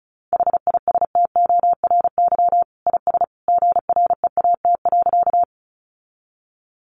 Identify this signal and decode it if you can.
Morse code, heard as a keyed tone, about 34 wpm, 700 Hz. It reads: HISTORY IS GREAT.
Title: Sentences from Top 600 Words 35wpm